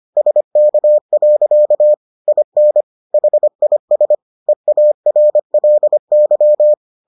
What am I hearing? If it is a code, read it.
Morse code, heard as a keyed tone, about 25 wpm, 600 Hz. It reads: SK. IN HIS EARLY